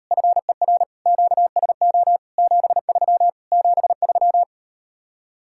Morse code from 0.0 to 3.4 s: FER QSO 73

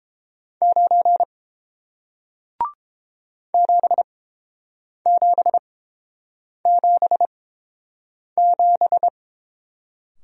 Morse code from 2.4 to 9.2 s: 7777

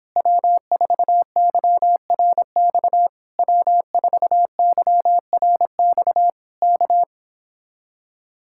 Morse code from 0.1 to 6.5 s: W4YRX W4YRX

